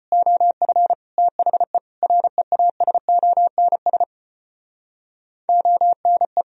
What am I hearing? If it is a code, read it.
Morse code, heard as a keyed tone, about 32 wpm, 700 Hz. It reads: OF THE REASONS ONE